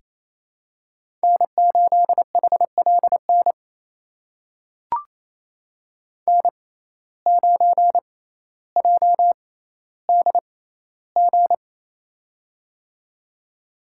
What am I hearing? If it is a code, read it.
Morse code, heard as a keyed tone, about 28 wpm, spaced as for 14 wpm, 700 Hz. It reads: N8HLN N9JDG
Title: USA Call Signs - 5 Characters - 6x Character Spacing - 28wpm